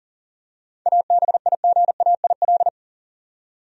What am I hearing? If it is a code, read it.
Morse code, heard as a keyed tone, about 40 wpm, 700 Hz. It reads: ABIGAIL
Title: USA Common Names - 40wpm